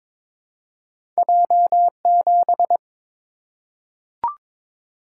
Morse code, heard as a keyed tone, about 22 wpm, 690 Hz. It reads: J7
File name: practice / 22wpm / Two Letter-Number Combinations - 22wpm